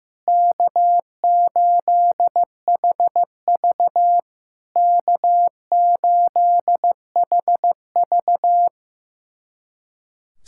Morse code, heard as a keyed tone, about 15 wpm, 690 Hz. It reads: K8HV K8HV
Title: SOTA QSO Practice 15wpm